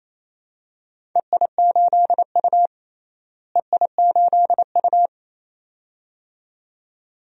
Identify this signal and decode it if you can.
Morse code, heard as a keyed tone, about 28 wpm, 700 Hz. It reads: EI8U EI8U
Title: DXCC-100 Call Signs - 4 Characters - Repeated 2x - 28wpm